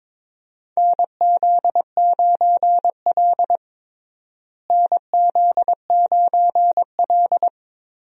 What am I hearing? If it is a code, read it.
Morse code, heard as a keyed tone, about 21 wpm, 700 Hz. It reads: NZ9L NZ9L
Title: USA Call Signs - 4 Characters - Repeated 3x - 22wpm